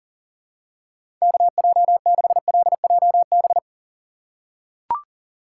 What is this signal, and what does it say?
Morse code, heard as a keyed tone, about 40 wpm, 700 Hz. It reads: KJ6LJB
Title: USA Call Signs - 6 Characters - 2x Character Spacing - 40wpm